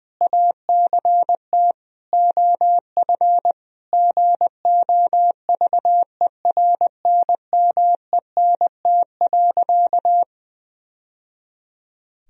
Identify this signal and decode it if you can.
Morse code, heard as a keyed tone, about 20 wpm, 700 Hz. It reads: ACT OF GOVERNMENT.
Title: Sentences from Top 1000 Words - Review Part 3 - 20wpm